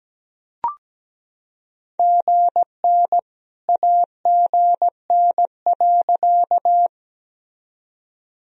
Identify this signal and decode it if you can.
Morse code, heard as a keyed tone, about 17 wpm, 700 Hz. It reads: GN AGN.
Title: Sentences from Top 300 Words and Abbreviations 17wpm